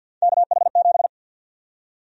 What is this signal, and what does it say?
Morse code, heard as a keyed tone, about 48 wpm, 700 Hz. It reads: KH7
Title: USA Call Sign Prefixes - 50wpm